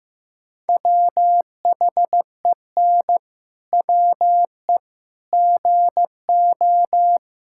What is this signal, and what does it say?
Morse code, heard as a keyed tone, about 16 wpm, 700 Hz. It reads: WHEN WE GO
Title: Pangram Challenge v2 - 15wpm